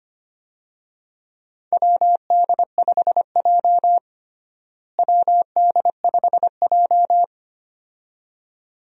Morse code, heard as a keyed tone, about 25 wpm, 700 Hz. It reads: WD5J WD5J
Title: USA Call Signs - 4 Characters - Repeated 2x - 25wpm